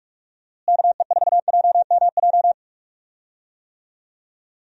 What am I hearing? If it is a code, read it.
Morse code, heard as a keyed tone, about 46 wpm, 700 Hz. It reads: KE4JMJ
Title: USA Call Signs - 6 Characters - Repeated 2x - 45wpm